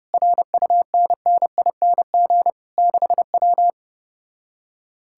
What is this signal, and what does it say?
Morse code, heard as a keyed tone, about 33 wpm, 700 Hz. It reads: RUNNING 6W